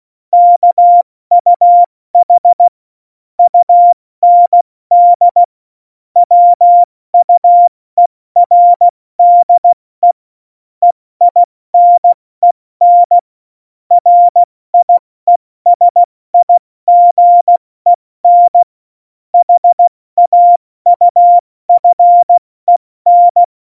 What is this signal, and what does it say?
Morse code, heard as a keyed tone, about 16 wpm, 700 Hz. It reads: KUH UND WUERDE EINEN RIESIGEN HAUFEN
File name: Morsekurs / Witze / Witz 26. Dezember 2017, 70 F80